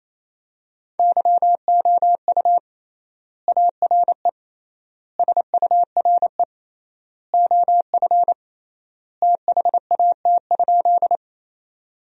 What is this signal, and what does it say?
Morse code, heard as a keyed tone, about 28 wpm, 700 Hz. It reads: YOU ARE SURE OF THAT?
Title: 3x Word-Spacing - Sentences from Top 300 Words - 28wpm